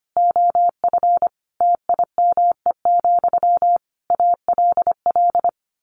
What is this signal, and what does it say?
Morse code, heard as a keyed tone, about 25 wpm, 700 Hz. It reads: OF TIME, ALL
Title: A Sound Of Thunder 25wpm